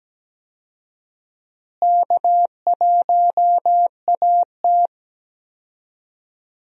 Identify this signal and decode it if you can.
Morse code, heard as a keyed tone, about 17 wpm, 700 Hz. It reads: K1AT
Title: USA Call Signs - 4 Characters - Repeated 3x - 17wpm